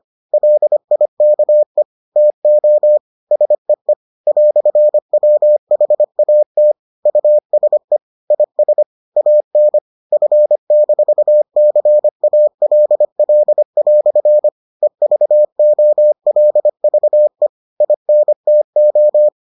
Morse code, heard as a keyed tone, about 25 wpm, 600 Hz. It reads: LIKE TO SEE "WHAT USE IS AN F-CALL" EVOLVE INTO